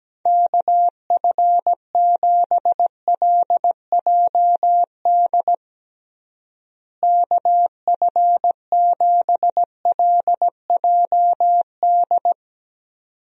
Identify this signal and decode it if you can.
Morse code, heard as a keyed tone, about 17 wpm, 700 Hz. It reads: KF7LJD KF7LJD
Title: USA Call Signs - 6 Characters - Repeated 2x - 17wpm